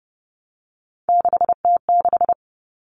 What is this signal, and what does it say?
Morse code, heard as a keyed tone, about 30 wpm, 700 Hz. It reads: 6T6
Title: All Cut Numbers - 3-Digit 30wpm